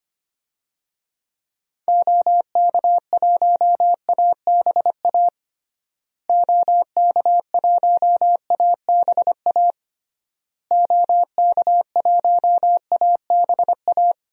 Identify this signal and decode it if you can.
Morse code, heard as a keyed tone, about 25 wpm, 700 Hz. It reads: OK1ABA OK1ABA OK1ABA
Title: DXCC-100 Call Signs - 6 Characters - Repeated 3x - 25wpm